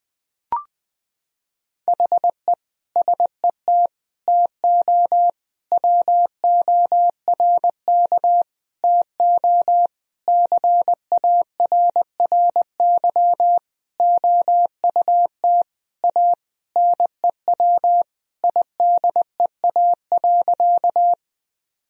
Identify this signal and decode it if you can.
Morse code, heard as a keyed tone, about 20 wpm, 700 Hz. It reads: HE SET TO WORK TO CARRY OUT A NEW IDEA.
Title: Sentences from Top 1000 Words - Review Part 1 - 20wpm